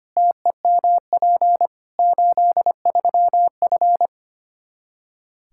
Morse code, heard as a keyed tone, about 25 wpm, 700 Hz. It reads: TEMP 83F
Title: K1USN SST QSO Practice 25wpm